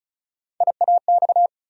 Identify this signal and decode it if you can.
Morse code, heard as a keyed tone, about 36 wpm, 700 Hz. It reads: IAX
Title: USA Call Sign Postfixes - 3 Letters 35wpm